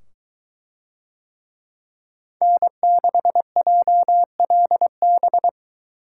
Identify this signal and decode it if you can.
Morse code, heard as a keyed tone, about 22 wpm, 700 Hz. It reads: N6JLB